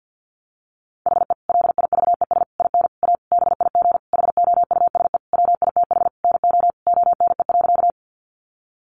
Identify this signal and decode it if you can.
Morse code, heard as a keyed tone, about 50 wpm, 700 Hz. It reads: HE LIVES IN A BIG HOUSE WITH NO ONE.
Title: Additional Easy Phrases 50wpm